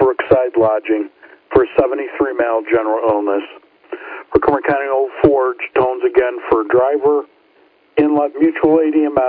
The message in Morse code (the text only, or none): none